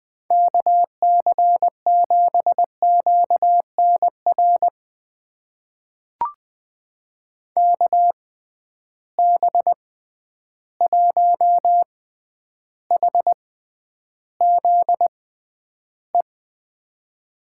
Morse code, heard as a keyed tone, about 21 wpm, spaced as for 10 wpm, 700 Hz. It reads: KC7QNR KB1HZE
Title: USA Call Signs - 6 Characters - 6x Character Spacing - 20wpm